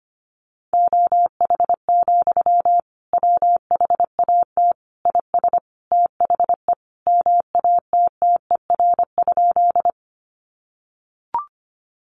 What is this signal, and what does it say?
Morse code, heard as a keyed tone, about 25 wpm, 700 Hz. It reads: OH, WHAT IS THE MATTER?
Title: Sentences from Top 700 Words 25wpm